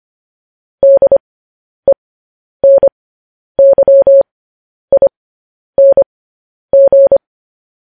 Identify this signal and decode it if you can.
Morse code, heard as a keyed tone, about 25 wpm, spaced as for 10 wpm, 550 Hz. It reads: DENYING
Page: news headlines in morse code at 10 wpm